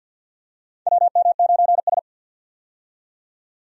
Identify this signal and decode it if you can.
Morse code, heard as a keyed tone, about 52 wpm, 700 Hz. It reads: WM9S
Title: USA Call Signs - 4 Characters - 50wpm